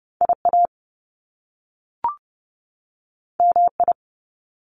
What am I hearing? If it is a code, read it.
Morse code, heard as a keyed tone, about 30 wpm, 700 Hz. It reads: IA MI
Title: Top Two Letter Combinations in Words 30wpm